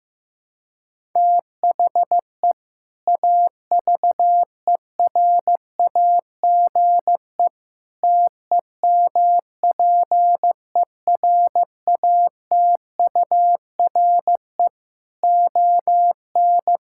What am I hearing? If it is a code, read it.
Morse code, heard as a keyed tone, about 15 wpm, 700 Hz. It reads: THE AVERAGE TEMPERATURE ON
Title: Sentences from Top 1000 Words - Review Part 6 - 15wpm